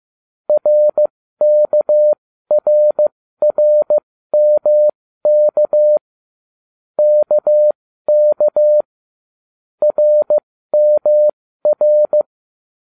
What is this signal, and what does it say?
Morse code, heard as a keyed tone, about 15 wpm, spaced as for 13 wpm, 600 Hz. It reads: RKRRMK KK RMR